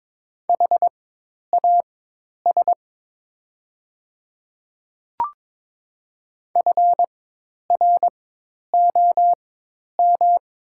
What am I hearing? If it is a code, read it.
Morse code, heard as a keyed tone, about 22 wpm, spaced as for 10 wpm, 700 Hz. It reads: HAS FROM
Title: 4x Character Spacing - Top 100 Words - 22wpm